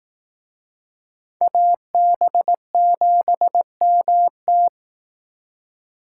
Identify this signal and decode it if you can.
Morse code, heard as a keyed tone, about 19 wpm, 700 Hz. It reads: AB7MT
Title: Speed Racing - USA Call Signs 15wpm